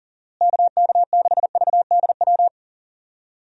Morse code, heard as a keyed tone, about 40 wpm, 700 Hz. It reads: KK6VDW
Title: USA Call Signs - 6 Characters - Repeated 3x - 40wpm